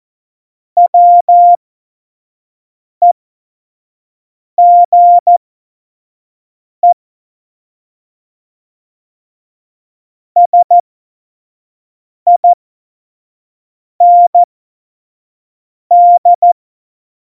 Morse code, heard as a keyed tone, about 14 wpm, spaced as for 5 wpm, 700 Hz. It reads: WEGE SIND
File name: Morsekurs / Witze / Witz 22. Dezember 2020, 25 F70